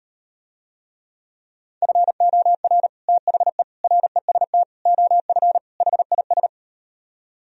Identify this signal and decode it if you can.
Morse code, heard as a keyed tone, about 38 wpm, 700 Hz. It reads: FOR THE REST OF HIS